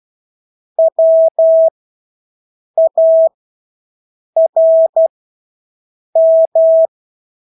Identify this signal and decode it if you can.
Morse code, heard as a keyed tone, about 12 wpm, spaced as for 6 wpm, 650 Hz. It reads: WARM
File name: Engelske ord 12-6 wpm